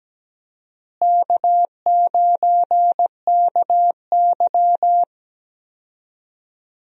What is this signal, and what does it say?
Morse code, heard as a keyed tone, about 17 wpm, 700 Hz. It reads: K9KY
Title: USA Call Signs - 4 Characters - Repeated 2x - 17wpm